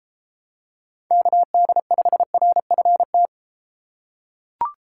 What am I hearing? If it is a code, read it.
Morse code, heard as a keyed tone, about 32 wpm, 700 Hz. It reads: KD5RFT